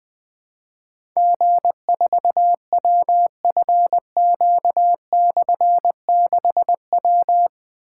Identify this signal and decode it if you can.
Morse code, heard as a keyed tone, about 20 wpm, 700 Hz. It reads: G4WFQ/6W
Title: Difficult Call Signs 20wpm